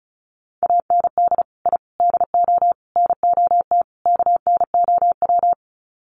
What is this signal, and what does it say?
Morse code, heard as a keyed tone, about 34 wpm, 700 Hz. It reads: AND I DO NOT KNOW
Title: Sets of 5 Words 35wpm